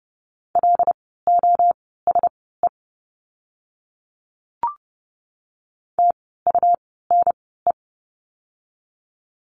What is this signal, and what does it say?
Morse code, heard as a keyed tone, about 30 wpm, spaced as for 17 wpm, 700 Hz. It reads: LOSE TUNE
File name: Lesson 17f - Letter u - Words 30wpm